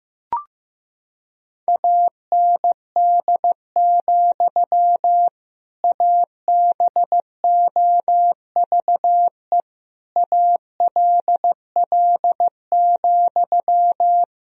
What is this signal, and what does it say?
Morse code, heard as a keyed tone, about 16 wpm, 700 Hz. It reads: AND, ABOVE ALL,
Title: Sentences from Top 1000 Words - Review Part 2 - 15wpm